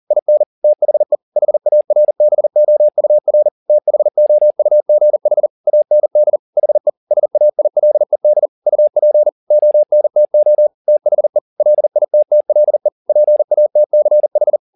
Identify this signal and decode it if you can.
Morse code, heard as a keyed tone, about 40 wpm, 600 Hz. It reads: IN THE HARBOUR THOUGH AND HE SAILED UP ONTO THE LITTLE PATCH
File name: the old man and the sea 2 40wpm